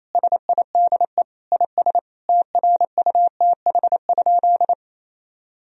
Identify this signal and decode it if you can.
Morse code, heard as a keyed tone, about 28 wpm, 700 Hz. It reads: SIDE IS TRUTH?
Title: Sentences from Top 1000 Words - Review Part 1 - 28wpm